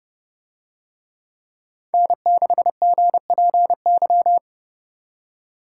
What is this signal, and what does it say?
Morse code, heard as a keyed tone, about 28 wpm, 700 Hz. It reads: N6GPY